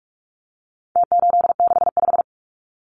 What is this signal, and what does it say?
Morse code, heard as a keyed tone, about 44 wpm, 700 Hz. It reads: T865
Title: Numbers - Contest Style - 4-Digit 45wpm